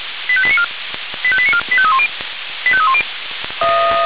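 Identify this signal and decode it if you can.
Five-tone selective call (ZVEI), heard as quick runs of five tones, tones of about 70 ms. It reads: 85894 85894 85310 85310
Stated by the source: ZVEI five-tone selective call 85320 received by SDR.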